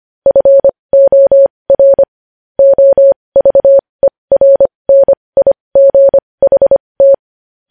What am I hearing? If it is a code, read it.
Morse code, heard as a keyed tone, about 25 wpm, spaced as for 20 wpm, 550 Hz. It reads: FOR OVERNIGHT
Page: news headlines in morse code at 20 wpm